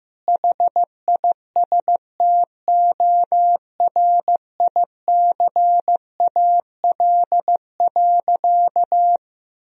Morse code, heard as a keyed tone, about 15 wpm, 700 Hz. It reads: HISTORICAL.
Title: Sentences from Top 1000 Words - Review Part 4 - 15wpm